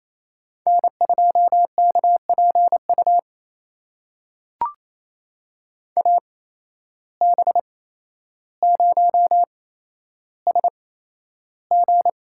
Morse code, heard as a keyed tone, about 28 wpm, spaced as for 12 wpm, 700 Hz. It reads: N2KPU AB0SG
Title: USA Call Signs - 5 Characters - 8x Character Spacing - 28wpm